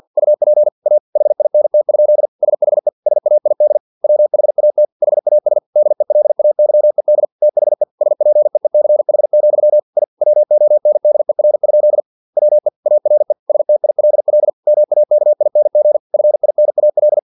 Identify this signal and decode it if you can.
Morse code, heard as a keyed tone, about 50 wpm, 600 Hz. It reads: UP A HINT? SHE SAID WHAT HAS DELAYED THE SPEECH, I WONDER? WE ARE STILL MAKING FINAL
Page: Never 49wpm